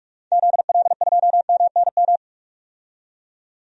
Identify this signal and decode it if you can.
Morse code, heard as a keyed tone, about 42 wpm, 700 Hz. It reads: ZL2MNM
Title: DXCC-100 Call Signs - 6 Characters - 45wpm